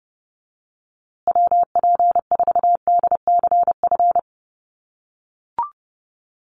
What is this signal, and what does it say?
Morse code, heard as a keyed tone, about 30 wpm, 700 Hz. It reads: WP4DCF
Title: USA Call Signs Repeated 2x 30wpm